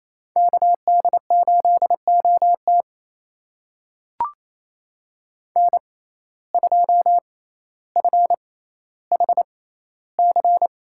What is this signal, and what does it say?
Morse code, heard as a keyed tone, about 28 wpm, spaced as for 14 wpm, 700 Hz. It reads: KD8OT N2FHC
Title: USA Call Signs - 5 Characters - 6x Character Spacing - 28wpm